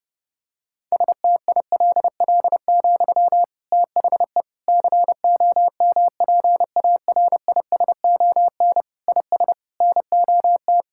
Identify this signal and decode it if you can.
Morse code, heard as a keyed tone, about 30 wpm, 700 Hz. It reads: STILL, THE COMPARISON IS NOT
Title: Sentences from Top 1600 Words 30wpm